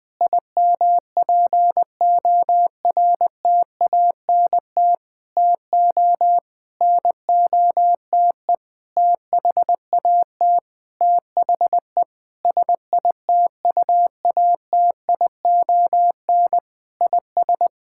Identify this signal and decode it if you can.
Morse code, heard as a keyed tone, about 20 wpm, 700 Hz. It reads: IMPORTANT TO NOTE THAT THE SITUATION IS
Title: Sentences with 8-Word Phrases - 20wpm